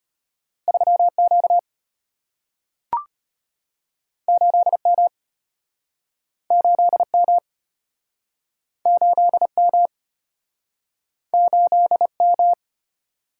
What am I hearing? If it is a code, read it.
Morse code, heard as a keyed tone, about 33 wpm, 700 Hz. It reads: 3Q 8M 8M 8M 8M